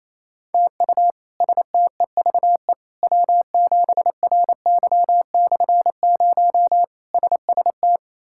Morse code, heard as a keyed tone, about 27 wpm, 700 Hz. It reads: TU STEVE W7RY/0 SST
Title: K1USN SST QSO Practice - 28wpm